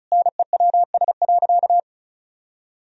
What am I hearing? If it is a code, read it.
Morse code, heard as a keyed tone, about 36 wpm, 700 Hz. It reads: NEWS.
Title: Sentences from Top 1000 Words - Review Part 4 - 35wpm